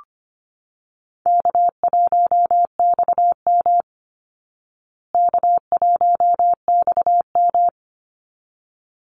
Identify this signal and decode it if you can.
Morse code, heard as a keyed tone, about 25 wpm, 700 Hz. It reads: K1XM K1XM
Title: Easy Call Signs Repeated 2x - 25wpm